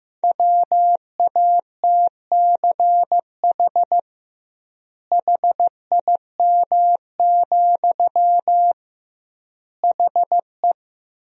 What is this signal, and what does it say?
Morse code, heard as a keyed tone, about 16 wpm, 700 Hz. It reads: WATCH HIM, HE
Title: 2x Word-Spacing - Sentences from Top 300 Words - 15wpm